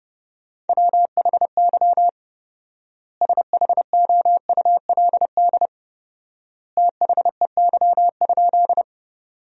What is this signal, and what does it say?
Morse code, heard as a keyed tone, about 30 wpm, 700 Hz. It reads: WHY SHOULD THEY?
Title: 4x Word-Spacing - Sentences from Top 300 Words - 30wpm